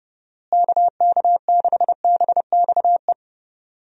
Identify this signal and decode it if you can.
Morse code, heard as a keyed tone, about 30 wpm, 700 Hz. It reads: KK6BXE
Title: USA Call Signs - 6 Characters - Repeated 3x - 30wpm